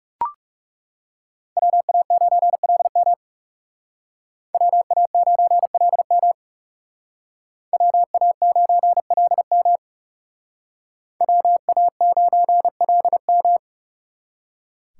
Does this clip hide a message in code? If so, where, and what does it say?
Yes: Morse code from 1.6 to 13.6 s, heard as a keyed tone, about 38 wpm, 700 Hz. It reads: WA9LM WA9LM WA9LM WA9LM